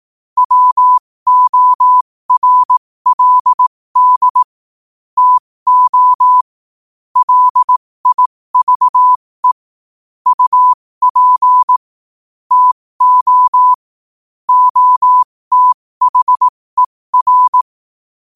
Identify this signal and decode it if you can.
Morse code, heard as a keyed tone, about 18 wpm, spaced as for 15 wpm, 1000 Hz. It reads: WORLD TO LIVE UP TO OTHER
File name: qotdfiles / 15wpm